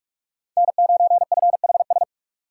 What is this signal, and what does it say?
Morse code, heard as a keyed tone, about 44 wpm, 690 Hz. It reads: N9FHS
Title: USA Call Signs - 5 Characters - 8x Character Spacing - 45wpm